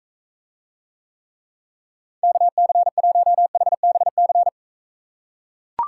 Morse code, heard as a keyed tone, about 40 wpm, 700 Hz. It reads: KC1HBC